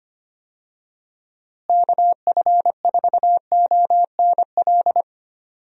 Morse code, heard as a keyed tone, about 25 wpm, 700 Hz. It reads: KF4ONL
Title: USA Call Signs - 6 Characters - Repeated 2x - 25wpm